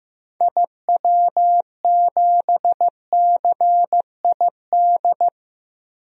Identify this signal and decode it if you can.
Morse code, heard as a keyed tone, about 16 wpm, 700 Hz. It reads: IW7CID
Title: DXCC-100 Call Signs - 6 Characters - Repeated 2x - 15wpm